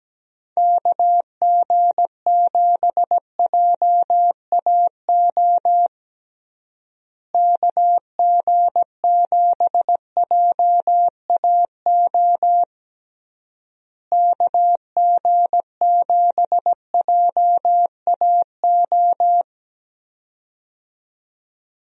Morse code, heard as a keyed tone, about 17 wpm, 700 Hz. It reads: KG7JAO KG7JAO KG7JAO
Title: USA Call Signs - 6 Characters - Repeated 3x - 17wpm